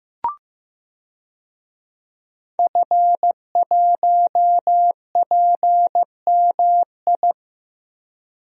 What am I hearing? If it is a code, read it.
Morse code, heard as a keyed tone, about 15 wpm, 700 Hz. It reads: F1PMI
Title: DXCC-100 Call Signs - 5 Characters - 15wpm